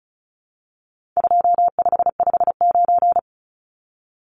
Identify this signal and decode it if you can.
Morse code, heard as a keyed tone, about 34 wpm, 700 Hz. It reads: 2559
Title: Numbers - 4-Digit 35wpm